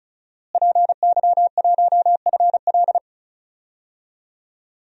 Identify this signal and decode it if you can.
Morse code, heard as a keyed tone, about 36 wpm, 700 Hz. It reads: PY1FL
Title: DXCC-100 Call Signs - 5 Characters - Repeated 2x - 35wpm